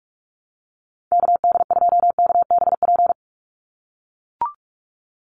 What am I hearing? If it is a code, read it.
Morse code, heard as a keyed tone, about 44 wpm, 690 Hz. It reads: KD2KBP
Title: USA Call Signs 45wpm